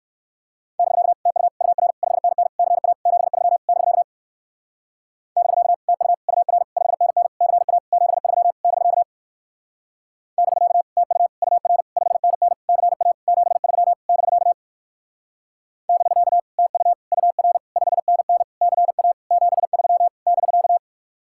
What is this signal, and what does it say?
Morse code, heard as a keyed tone, about 59 wpm, 700 Hz. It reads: <BK> TU UR 5NN CA 73 <BK> <BK> TU UR 5NN CA 73 <BK> <BK> TU UR 5NN CA 73 <BK> <BK> TU UR 5NN CA 73 <BK>